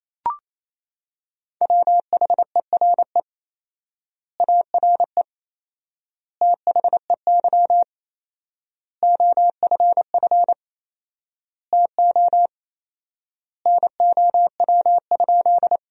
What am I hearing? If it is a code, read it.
Morse code, heard as a keyed tone, about 28 wpm, 700 Hz. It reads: WHERE ARE THEY OFF TO NOW?
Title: 4x Word-Spacing - Sentences from Top 200 Words - 28wpm